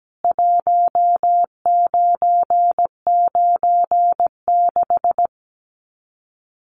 Morse code, heard as a keyed tone, about 17 wpm, 700 Hz. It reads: 1996
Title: Common QSO Phrases 17wpm